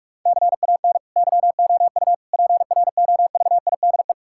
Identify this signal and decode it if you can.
Morse code, heard as a keyed tone, about 46 wpm, 700 Hz. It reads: CAN YOU PROVIDE
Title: Sentences from Top 1000 Words - Review Part 5 - 45wpm